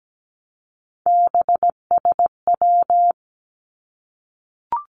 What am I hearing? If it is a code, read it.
Morse code, heard as a keyed tone, about 17 wpm, 700 Hz. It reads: BSW
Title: Three Challenging Letter-Number Combinations 17wpm